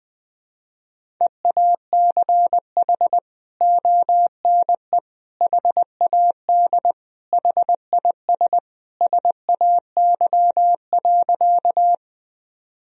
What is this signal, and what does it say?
Morse code, heard as a keyed tone, about 20 wpm, 700 Hz. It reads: EACH ONE HAD HIS SAY.
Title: Sentences from Top 1000 Words - Review Part 1 - 20wpm